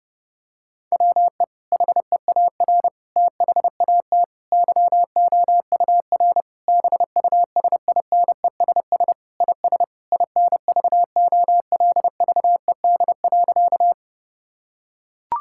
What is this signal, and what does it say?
Morse code, heard as a keyed tone, about 30 wpm, 700 Hz. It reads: WE HEAR THAT YOUR BUSINESS IS INVOLVED.
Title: Sentences from Top 1000 Words - Review Part 5 - 30wpm